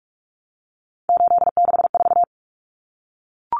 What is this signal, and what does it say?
Morse code, heard as a keyed tone, about 44 wpm, 700 Hz. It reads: 864
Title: Common Cut Numbers - 3-Digit 45wpm